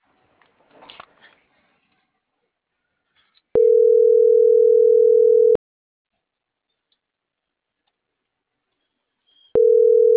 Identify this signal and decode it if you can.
Ringback tone (North America), 440 + 480 Hz.